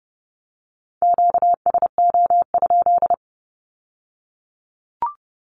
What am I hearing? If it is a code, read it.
Morse code, heard as a keyed tone, about 30 wpm, 700 Hz. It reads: QSO?
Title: Q-Codes with Spoken Meaning - 30wpm